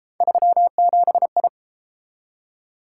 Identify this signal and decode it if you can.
Morse code, heard as a keyed tone, about 32 wpm, 700 Hz. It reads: 37I